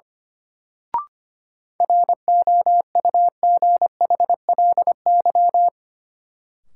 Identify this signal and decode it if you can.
Morse code, heard as a keyed tone, about 25 wpm, 700 Hz. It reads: ROUGHLY
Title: Sentences from Top 1500 Words - New Word Drill Alt - 25wpm